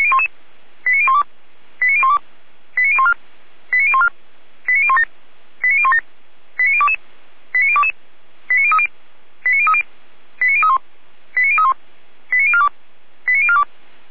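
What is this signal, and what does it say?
Five-tone selective call (ZVEI), heard as quick runs of five tones, tones of about 70 ms. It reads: 89012 89012 89014 89014 89017 89017 89022 89022 89030 89030 89031 89031 89042 89042
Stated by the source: ZVEI five-tone call 89020 received on SDR.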